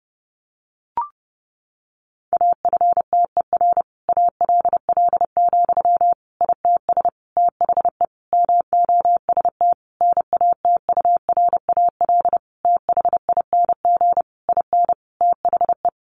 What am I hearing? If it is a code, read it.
Morse code, heard as a keyed tone, about 30 wpm, 700 Hz. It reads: AFTER ALL, ITS THE MOST NATURAL THING IN THE
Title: Sentences from Top 700 Words 30wpm